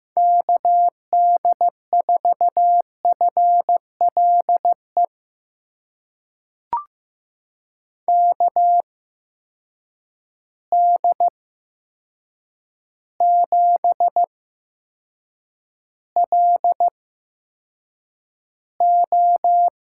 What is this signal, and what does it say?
Morse code, heard as a keyed tone, about 15 wpm, spaced as for 7 wpm, 700 Hz. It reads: KD4FLE KD7LO
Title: USA Call Signs - 6 Characters - 8x Character Spacing - 15wpm